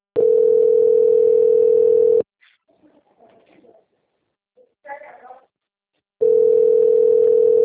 Ringback tone (North America), 440 + 480 Hz.
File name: rg-1600-unknown- - - .10831